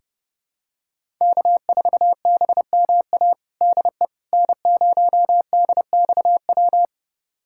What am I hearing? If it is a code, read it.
Morse code, heard as a keyed tone, about 32 wpm, 700 Hz. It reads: K4BMA DE N0DXW
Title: USA Callsign DE Callsign - 5 Characters - 30wpm